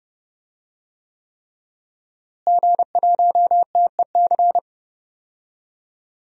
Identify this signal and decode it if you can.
Morse code, heard as a keyed tone, about 31 wpm, 700 Hz. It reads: G1TEC
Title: DXCC-100 Call Signs - 5 Characters - Repeated 3x - 30wpm